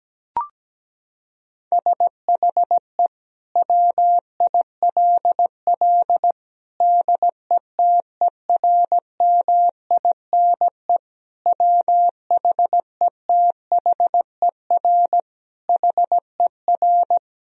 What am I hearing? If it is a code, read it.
Morse code, heard as a keyed tone, about 17 wpm, 700 Hz. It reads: SHE WILL DETERMINE WHETHER HER
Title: Sentences from Top 1000 Words - Review Part 4 - 17wpm